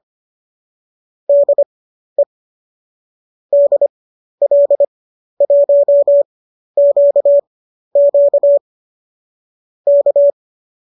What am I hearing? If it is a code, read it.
Morse code, heard as a keyed tone, about 25 wpm, spaced as for 12 wpm, 590 Hz. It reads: DE DL1QQ K
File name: qso012 12